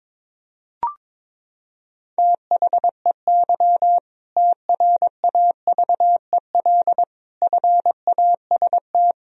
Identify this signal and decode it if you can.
Morse code, heard as a keyed tone, about 22 wpm, 700 Hz. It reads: THEY TRAVEL FAST
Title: Sentences from Top 1000 Words - Review Part 3 - 22wpm